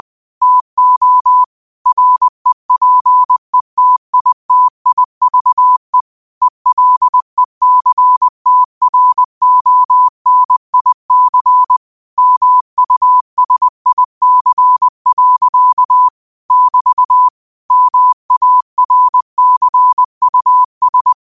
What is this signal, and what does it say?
Morse code, heard as a keyed tone, about 20 wpm, 1000 Hz. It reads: TO REPETITIVE ELECTRONIC MUSIC. <BT> MARCUS
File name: qotdfiles / 20wpm